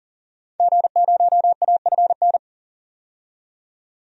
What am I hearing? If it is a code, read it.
Morse code, heard as a keyed tone, about 40 wpm, 700 Hz. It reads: G0AFN
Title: DXCC-100 Call Signs - 5 Characters - 40wpm